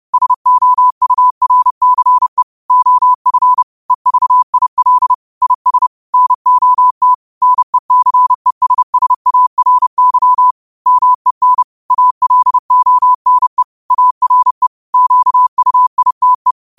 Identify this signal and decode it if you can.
Morse code, heard as a keyed tone, about 30 wpm, 1000 Hz. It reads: SOURCE OF EVIL IS NOT NECESSARY MEN ALONE ARE QUITE